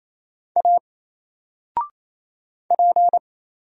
Morse code, heard as a keyed tone, about 27 wpm, 700 Hz. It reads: A P
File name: Single-Letter-Number-28wpm